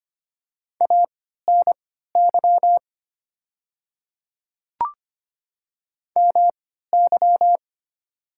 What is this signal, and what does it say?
Morse code, heard as a keyed tone, about 25 wpm, spaced as for 14 wpm, 690 Hz. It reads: ANY MY
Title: 3x Character Spacing - Top 200 Words - 25wpm